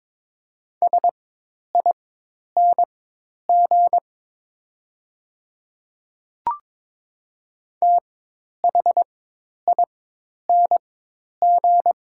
Morse code, heard as a keyed tone, about 22 wpm, spaced as for 10 wpm, 700 Hz. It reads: SING THING
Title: 4x Character Spacing - Top 500 Words - 22wpm